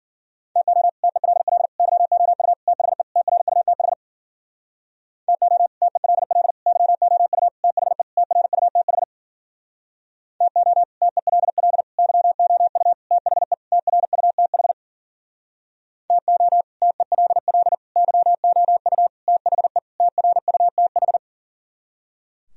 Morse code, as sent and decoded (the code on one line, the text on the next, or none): - --- / - . .-.. .-.. / -.-- --- ..- / - .... / - .-. ..- - .... / - --- / - . .-.. .-.. / -.-- --- ..- / - .... / - .-. ..- - .... / - --- / - . .-.. .-.. / -.-- --- ..- / - .... / - .-. ..- - .... / - --- / - . .-.. .-.. / -.-- --- ..- / - .... / - .-. ..- - ....
TO TELL YOU THE TRUTH TO TELL YOU THE TRUTH TO TELL YOU THE TRUTH TO TELL YOU THE TRUTH